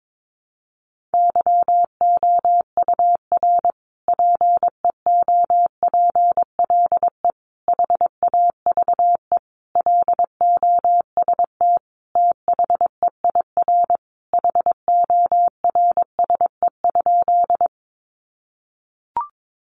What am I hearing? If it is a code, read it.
Morse code, heard as a keyed tone, about 22 wpm, 700 Hz. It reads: YOUR PEOPLE HAVE LOST THEIR HORSE?
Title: Sentences from Top 1400 Words 22wpm